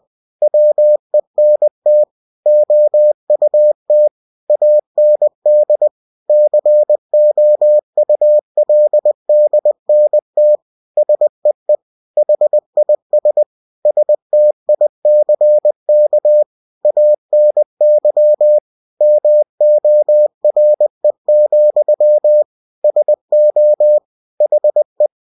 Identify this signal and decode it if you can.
Morse code, heard as a keyed tone, about 20 wpm, 600 Hz. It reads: WENT OUT AND COULDNT SEE HIS STICK ANY MORE, SO HE
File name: cw-books / winnie-the-pooh / winnie-20wpm-page-0096